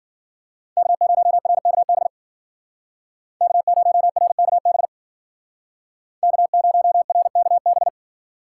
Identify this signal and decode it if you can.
Morse code, heard as a keyed tone, about 52 wpm, 700 Hz. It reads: K0RKB K0RKB K0RKB